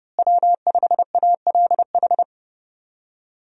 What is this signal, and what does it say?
Morse code, heard as a keyed tone, about 30 wpm, 700 Hz. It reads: W5ALH
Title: USA Call Signs - 5 Characters - Repeated 3x - 30wpm